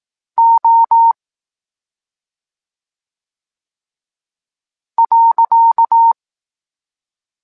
Morse code, heard as a keyed tone, about 18 wpm, 920 Hz. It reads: O .
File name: Tekst-26 3